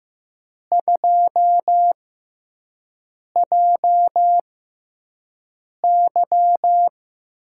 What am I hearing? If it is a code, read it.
Morse code, heard as a keyed tone, about 15 wpm, spaced as for 5 wpm, 700 Hz. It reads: 2JY
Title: USA Call Signs - 6 Characters - 6x Character Spacing - 15wpm